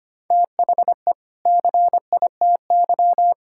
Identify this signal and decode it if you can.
Morse code, heard as a keyed tone, about 25 wpm, 700 Hz. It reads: THE CITY
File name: practice / 25wpm / Difficult Sentences with Hints - 25wpm